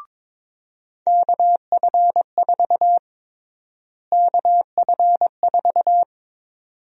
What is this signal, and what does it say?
Morse code, heard as a keyed tone, about 22 wpm, 700 Hz. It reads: KF4 KF4
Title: USA Call Sign Prefixes - Repeated 2x - 22wpm